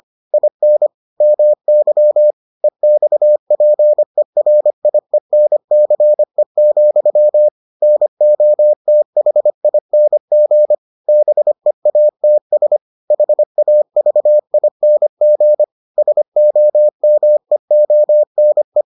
Morse code, heard as a keyed tone, about 25 wpm, 600 Hz. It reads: IN MY EXPERIENCE, NOTHING BEATS HAVING SOMEONE